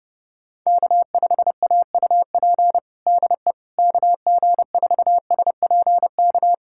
Morse code, heard as a keyed tone, about 30 wpm, 700 Hz. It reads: K5AUP DE KG4SPK
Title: USA Callsign DE Callsign 30wpm